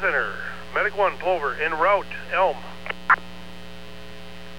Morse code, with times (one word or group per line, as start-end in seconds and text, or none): none